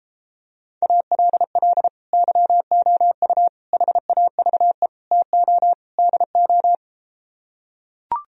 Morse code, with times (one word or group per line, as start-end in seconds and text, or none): 0.8-1.9 ALL
2.1-3.5 YOU
3.7-4.9 HAVE
5.1-5.7 TO
6.0-6.8 DO